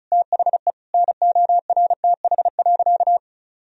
Morse code, heard as a keyed tone, about 36 wpm, 700 Hz. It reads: THE NORTH.
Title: Sentences from Top 1000 Words - Review Part 1 - 35wpm